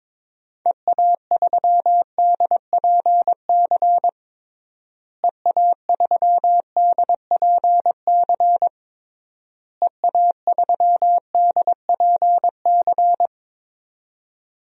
Morse code, heard as a keyed tone, about 22 wpm, 700 Hz. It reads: EA3DPC EA3DPC EA3DPC
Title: DXCC-100 Call Signs - 6 Characters - Repeated 3x - 22wpm